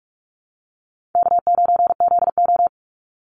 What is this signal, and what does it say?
Morse code, heard as a keyed tone, about 44 wpm, 700 Hz. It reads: K9ZO
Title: Call Signs - 45wpm